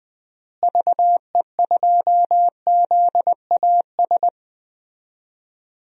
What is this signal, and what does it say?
Morse code, heard as a keyed tone, about 20 wpm, 700 Hz. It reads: VE2ZAS
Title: DXCC-100 Call Signs - 6 Characters - Repeated 2x - 20wpm